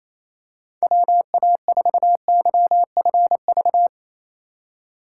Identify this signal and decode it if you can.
Morse code, heard as a keyed tone, about 28 wpm, 700 Hz. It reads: WA4YFV
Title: USA Call Signs - 6 Characters - 4x Character Spacing - 28wpm